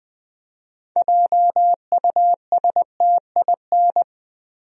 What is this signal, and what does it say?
Morse code, heard as a keyed tone, about 20 wpm, 700 Hz. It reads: JUSTIN
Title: USA Common Names - 20wpm